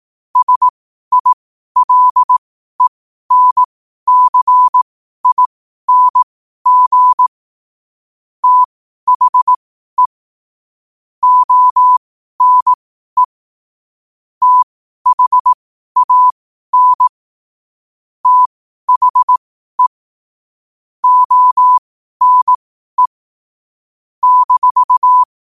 Morse code, heard as a keyed tone, about 18 wpm, spaced as for 12 wpm, 1000 Hz. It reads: SILENCING THE ONE THAN THE ONE -
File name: qotdfiles / 12wpm